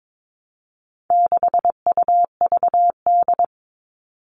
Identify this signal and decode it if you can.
Morse code, heard as a keyed tone, about 22 wpm, 700 Hz. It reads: 6UVD
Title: All Cut Numbers - 4-Digit 22wpm